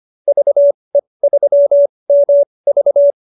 Morse code, heard as a keyed tone, about 25 wpm, spaced as for 20 wpm, 590 Hz. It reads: VE3MV
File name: qso220b 20